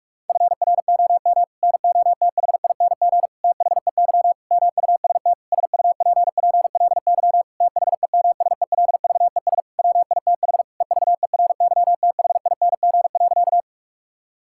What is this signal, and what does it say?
Morse code, heard as a keyed tone, about 46 wpm, 700 Hz. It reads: FROM NOTHING THEY MUST SUPPLY THEMSELVES WITH EVERYTHING.
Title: Sentences from Top 1000 Words - Review Part 5 - 45wpm